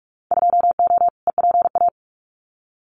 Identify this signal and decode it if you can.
Morse code, heard as a keyed tone, about 46 wpm, 700 Hz. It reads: 2O EPA
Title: Winter Field Day Exchanges Encore 45wpm